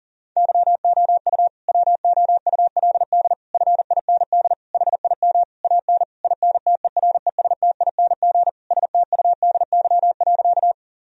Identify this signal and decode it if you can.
Morse code, heard as a keyed tone, about 37 wpm, 700 Hz. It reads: YOU WOULD FIND HIM AN INTERESTING STUDY.
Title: Sentences from Top 1000 Words - Review Part 1 - 40wpm